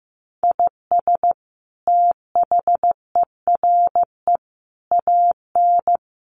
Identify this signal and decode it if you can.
Morse code, heard as a keyed tone, about 15 wpm, 700 Hz. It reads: IS THERE AN
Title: Sentences from Top 1200 Words 15wpm